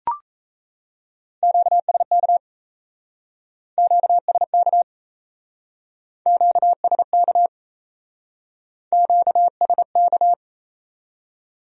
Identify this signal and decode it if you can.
Morse code, heard as a keyed tone, about 38 wpm, 700 Hz. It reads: QSK QSK QSK QSK